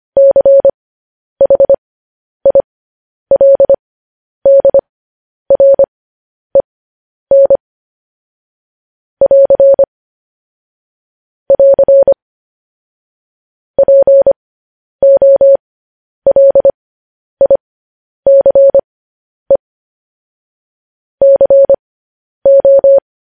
Morse code, heard as a keyed tone, about 25 wpm, spaced as for 10 wpm, 550 Hz. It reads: CHILDREN <AR> <AR> POLICE CO
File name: podcast / News headlines-1x-25x10